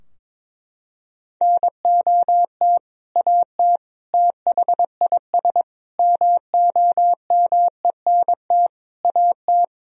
Morse code, heard as a keyed tone, about 21 wpm, 700 Hz. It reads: NOT AT THIS MOMENT AT
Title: Sentences from Top 1000 Words - Review Part 4 - 22wpm